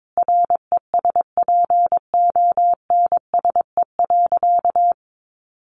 Morse code, heard as a keyed tone, about 22 wpm, 700 Hz. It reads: RESPONSE.
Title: Sentences from Top 1400 Words 22wpm